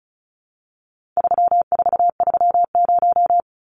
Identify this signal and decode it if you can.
Morse code, heard as a keyed tone, about 34 wpm, 700 Hz. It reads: 3430
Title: Numbers - 4-Digit 35wpm